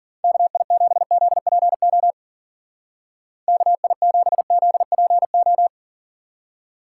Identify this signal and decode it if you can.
Morse code, heard as a keyed tone, about 48 wpm, 700 Hz. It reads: KI7ZPO KI7ZPO
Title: Speed Racing - USA Call Signs 40wpm